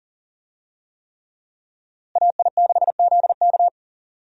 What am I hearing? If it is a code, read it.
Morse code, heard as a keyed tone, about 40 wpm, 700 Hz. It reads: AI6ZK
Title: USA Call Signs - 5 Characters - Repeated 3x - 40wpm